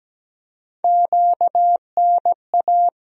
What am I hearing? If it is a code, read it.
Morse code, heard as a keyed tone, about 17 wpm, 700 Hz. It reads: QNA
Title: USA Call Sign Postfixes - 3 Letters 17wpm